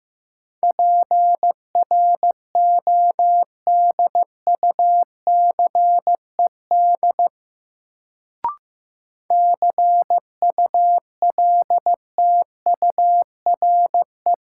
Morse code, heard as a keyed tone, about 15 wpm, 700 Hz. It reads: PRODUCED CULTURE
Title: Sentences from Top 1000 Words - Review Part 6 - New Word Drill - 15wpm